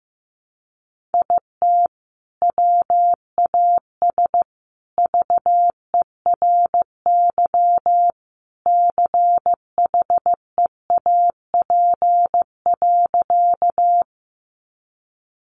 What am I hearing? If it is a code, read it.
Morse code, heard as a keyed tone, about 14 wpm, 700 Hz. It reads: IT WAS VERY CHEAP.
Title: Sentences from Top 1200 Words 15wpm